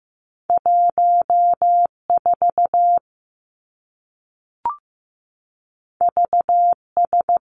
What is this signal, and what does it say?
Morse code, heard as a keyed tone, about 15 wpm, 700 Hz. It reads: 1 4 V S
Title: Two Challenging Letter-Number Combinations 15wpm